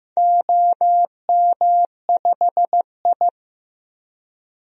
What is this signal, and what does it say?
Morse code, heard as a keyed tone, about 15 wpm, 700 Hz. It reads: OM5I
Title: DXCC-100 Call Signs - 4 Characters - Repeated 3x - 15wpm